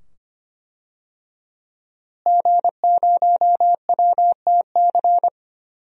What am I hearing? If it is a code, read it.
Morse code, heard as a keyed tone, about 25 wpm, 700 Hz. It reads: G0WTC
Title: DXCC-100 Call Signs - 5 Characters - Repeated 2x - 25wpm